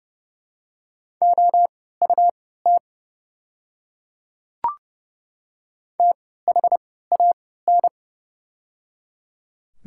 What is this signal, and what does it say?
Morse code, heard as a keyed tone, about 30 wpm, spaced as for 17 wpm, 700 Hz. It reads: OUT THAN
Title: 3x Character Spacing - Top 100 Words - 30wpm